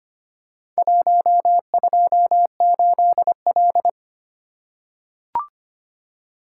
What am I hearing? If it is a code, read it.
Morse code, heard as a keyed tone, about 25 wpm, 700 Hz. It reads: 128L
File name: practice / 25wpm / Four Letter-Number Combinations - 25wpm